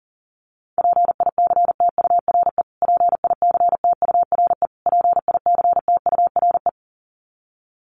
Morse code, heard as a keyed tone, about 40 wpm, 700 Hz. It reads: PICTURE PICTURE PICTURE
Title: Top 300 Words Repeated 40wpm